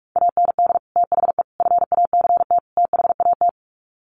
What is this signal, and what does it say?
Morse code, heard as a keyed tone, about 44 wpm, 700 Hz. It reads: AND THE FACT THAT
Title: Sets of 4 Words 45wpm